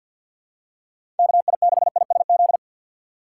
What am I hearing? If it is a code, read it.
Morse code, heard as a keyed tone, about 52 wpm, 690 Hz. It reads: KI6ISZ
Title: USA Call Signs - 6 Characters - 3x Character Spacing - 50wpm